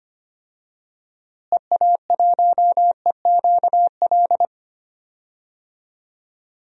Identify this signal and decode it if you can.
Morse code, heard as a keyed tone, about 25 wpm, 700 Hz. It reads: EA1EQL
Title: DXCC-100 Call Signs - 6 Characters - Repeated 2x - 25wpm